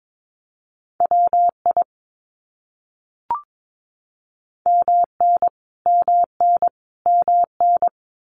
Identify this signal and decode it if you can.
Morse code, heard as a keyed tone, about 22 wpm, 700 Hz. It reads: WI MN MN MN
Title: US States Repeated 3x 22wpm